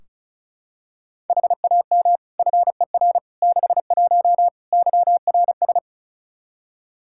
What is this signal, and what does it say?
Morse code, heard as a keyed tone, about 36 wpm, 700 Hz. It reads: HAM FER 61 YRS
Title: Practice QSO - Exchange 3 - More Info - 35wpm